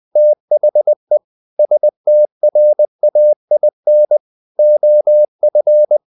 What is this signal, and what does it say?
Morse code, heard as a keyed tone, about 20 wpm, 600 Hz. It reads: THE STRAIN OF
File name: the old man and the sea / the old man and the sea part 1 20wpm